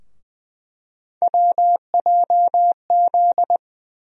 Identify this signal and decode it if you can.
Morse code, heard as a keyed tone, about 21 wpm, 700 Hz. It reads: WJZ
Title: USA Call Sign Postfixes - 3 Letters 20wpm